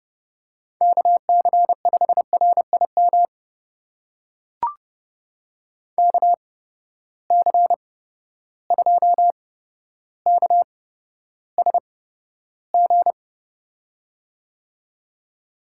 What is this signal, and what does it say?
Morse code, heard as a keyed tone, about 30 wpm, spaced as for 13 wpm, 700 Hz. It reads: KC5RIM KC2KSG
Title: USA Call Signs - 6 Characters - 8x Character Spacing - 30wpm